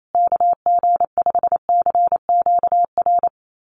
Morse code, heard as a keyed tone, about 28 wpm, 700 Hz. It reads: KG5CQR
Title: USA Call Signs Repeated 3x 28wpm